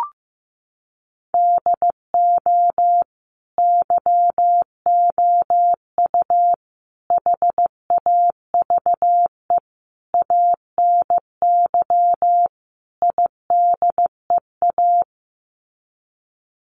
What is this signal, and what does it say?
Morse code, heard as a keyed tone, about 15 wpm, 700 Hz. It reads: DO YOU HAVE ANY IDEA
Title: Sets of 5 Words Encore 15wpm